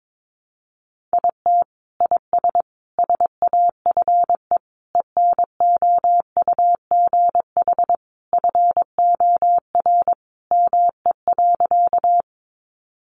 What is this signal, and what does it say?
Morse code, heard as a keyed tone, about 22 wpm, 700 Hz. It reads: IT IS SAFE ENOUGH FOR ME.
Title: Sentences from Top 900 Words 22wpm